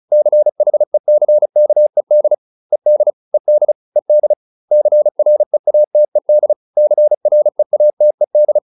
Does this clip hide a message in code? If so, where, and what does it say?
Yes: Morse code from 0.1 to 8.6 s, heard as a keyed tone, about 34 wpm, 600 Hz. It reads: CHECKED ED ED ED CREATED CREATED